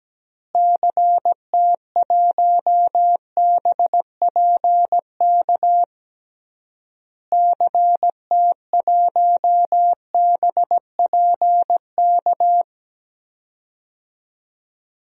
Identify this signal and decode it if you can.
Morse code, heard as a keyed tone, about 17 wpm, 700 Hz. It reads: CT1BPK CT1BPK
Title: DXCC-100 Call Signs - 6 Characters - Repeated 3x - 17wpm